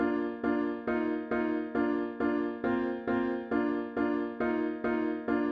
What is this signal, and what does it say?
There is no code.